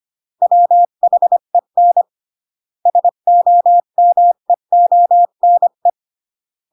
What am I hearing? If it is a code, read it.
Morse code, heard as a keyed tone, about 25 wpm, 700 Hz. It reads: WHEN SOMEONE